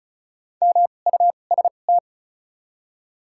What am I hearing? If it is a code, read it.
Morse code, heard as a keyed tone, about 34 wpm, spaced as for 25 wpm, 700 Hz. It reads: MUST
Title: 2x Character Spacing - Top 300 Words - 35wpm